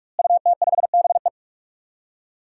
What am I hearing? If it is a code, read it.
Morse code, heard as a keyed tone, about 44 wpm, 700 Hz. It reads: UT5BE
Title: DXCC-100 Call Signs - 5 Characters - Repeated 2x - 45wpm